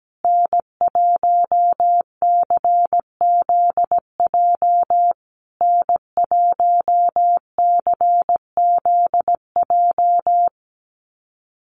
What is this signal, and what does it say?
Morse code, heard as a keyed tone, about 17 wpm, 700 Hz. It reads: N1CZJ N1CZJ
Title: USA Call Signs Repeated 3x 17wpm